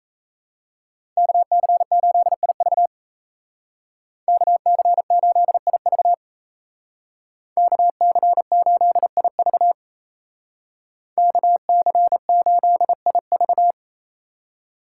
Morse code, heard as a keyed tone, about 33 wpm, 700 Hz. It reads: KC8IV KC8IV KC8IV KC8IV
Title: Speed Racing - USA 2x2 Call Signs 28wpm